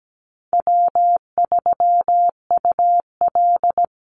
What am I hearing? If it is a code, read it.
Morse code, heard as a keyed tone, about 17 wpm, 700 Hz. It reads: W3UL
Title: Easy Call Signs Repeated 3x - 17wpm